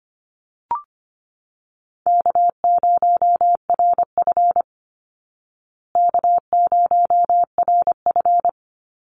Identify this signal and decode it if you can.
Morse code, heard as a keyed tone, about 25 wpm, 700 Hz. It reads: K0RF K0RF
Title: Easy Call Signs Repeated 3x - 25wpm